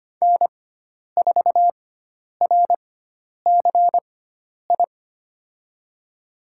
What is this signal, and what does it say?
Morse code, heard as a keyed tone, about 25 wpm, spaced as for 10 wpm, 700 Hz. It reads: N4RCI
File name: Lesson 16g - Letter c - Call Signs 25wpm